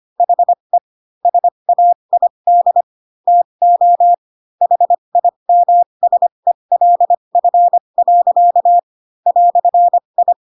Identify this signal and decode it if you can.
Morse code, heard as a keyed tone, about 25 wpm, spaced as for 22 wpm, 700 Hz. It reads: HE SAID TO HIMSELF. "I
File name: Pinocchio / Pinocchio ch3 18 25 ik1hge